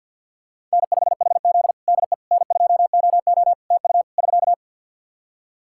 Morse code, heard as a keyed tone, about 47 wpm, 700 Hz. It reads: N5HZ DE N2OO TU <SK>
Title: Practice QSO - Exchange 4 - Ending - 50wpm